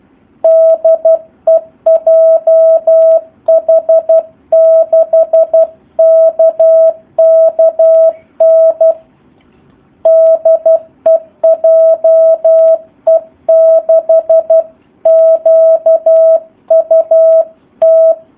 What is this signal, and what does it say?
Morse code, heard as a keyed tone, about 12 wpm, 650 Hz. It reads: DE JH6KKN DE JE6QUT
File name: jh8cbh / 2019cw / SP04- w